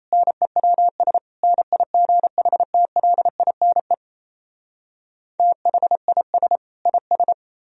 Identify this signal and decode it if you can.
Morse code, heard as a keyed tone, about 38 wpm, 700 Hz. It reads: NEWS NIGHTLINE THIS IS